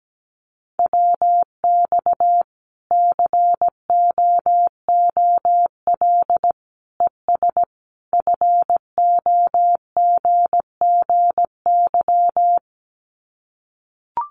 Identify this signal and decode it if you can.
Morse code, heard as a keyed tone, about 17 wpm, 700 Hz. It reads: WX COOL ES FOGGY
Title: Common QSO Phrases 17wpm